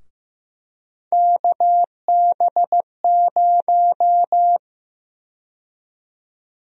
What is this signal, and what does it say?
Morse code, heard as a keyed tone, about 15 wpm, 700 Hz. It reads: KB0
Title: USA Call Sign Prefixes - Repeated 2x - 15wpm